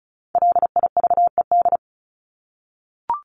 Morse code, heard as a keyed tone, about 34 wpm, 700 Hz. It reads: LIVED
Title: ED Suffix - Top 100 Words with Base Words 35wpm